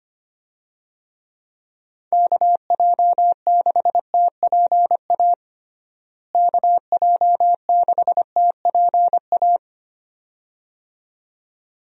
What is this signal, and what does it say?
Morse code, heard as a keyed tone, about 25 wpm, 700 Hz. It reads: KJ6TPA KJ6TPA
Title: USA Call Signs - 6 Characters - Repeated 2x - 25wpm